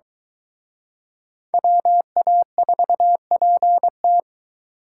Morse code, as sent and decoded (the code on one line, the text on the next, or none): .-- .- ....- .--. -
WA4PT